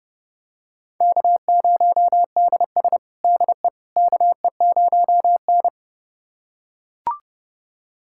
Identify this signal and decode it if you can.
Morse code, heard as a keyed tone, about 30 wpm, 700 Hz. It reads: K0DS DE KE0N
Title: USA Callsign DE Callsign - 4 Characters - 30wpm